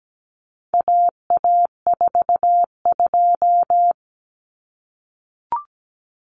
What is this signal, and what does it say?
Morse code, heard as a keyed tone, about 17 wpm, 700 Hz. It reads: AA42
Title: Common Cut Numbers - 4-Digit 17wpm